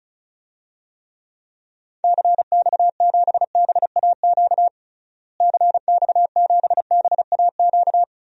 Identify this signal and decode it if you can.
Morse code, heard as a keyed tone, about 34 wpm, 700 Hz. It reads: CX7BAQ CX7BAQ
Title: DXCC-100 Call Signs - 6 Characters - Repeated 2x - 35wpm